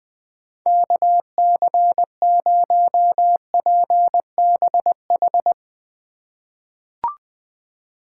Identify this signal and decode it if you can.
Morse code, heard as a keyed tone, about 20 wpm, 700 Hz. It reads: KC0PBH
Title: USA Call Signs - 6 Characters - 2x Character Spacing - 20wpm